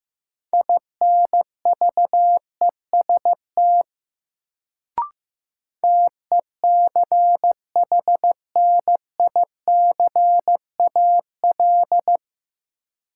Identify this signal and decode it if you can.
Morse code, heard as a keyed tone, about 15 wpm, 700 Hz. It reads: INVEST TECHNICAL
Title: Sentences from Top 1100 Words - New Word Drill Alt - 15wpm